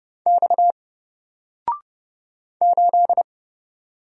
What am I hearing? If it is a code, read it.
Morse code, heard as a keyed tone, about 29 wpm, 700 Hz. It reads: X 8
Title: Single Character (Letter:Number:Punctuation:Prosign) 30wpm